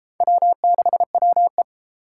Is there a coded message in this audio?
Yes: Morse code from 0.2 to 1.6 s, heard as a keyed tone, about 28 wpm, 700 Hz. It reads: W6WE